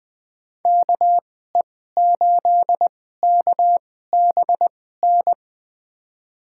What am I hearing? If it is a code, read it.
Morse code, heard as a keyed tone, about 21 wpm, spaced as for 15 wpm, 700 Hz. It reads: KE8KBN
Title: USA Call Signs - 2x Character Spacing - 20wpm